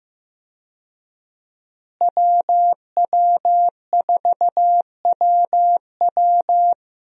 Morse code, heard as a keyed tone, about 15 wpm, 700 Hz. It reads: WW4WW
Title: USA Call Signs - 5 Characters - Repeated 2x - 15wpm